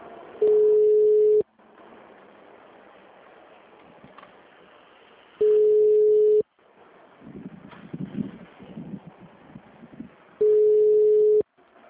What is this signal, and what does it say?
Ringback tone (Europe, 425 Hz).
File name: out- -2002- - - .58